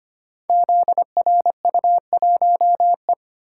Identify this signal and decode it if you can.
Morse code, heard as a keyed tone, about 25 wpm, 700 Hz. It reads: ZRU1E
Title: Code Groups - 5 Letter:Numbers - 25wpm